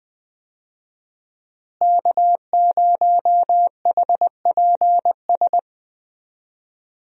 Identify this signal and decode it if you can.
Morse code, heard as a keyed tone, about 20 wpm, 700 Hz. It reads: K0HPS
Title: USA Call Signs - 5 Characters - Repeated 3x - 20wpm